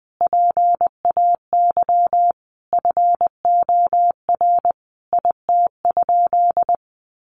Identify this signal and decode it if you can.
Morse code, heard as a keyed tone, about 19 wpm, 700 Hz. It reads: PAY FOR IT?
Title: Sentences from Top 1300 Words 20wpm